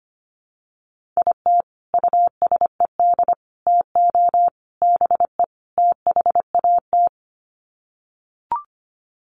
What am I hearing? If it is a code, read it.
Morse code, heard as a keyed tone, about 25 wpm, 700 Hz. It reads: IT USED TO BE THAT
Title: Sets of 5 Words Encore2 25wpm